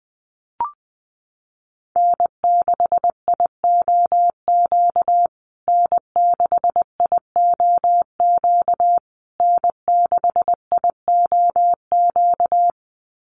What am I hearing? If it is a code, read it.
Morse code, heard as a keyed tone, about 20 wpm, 700 Hz. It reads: N6IOQ N6IOQ N6IOQ
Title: USA Call Signs Repeated 3x 20wpm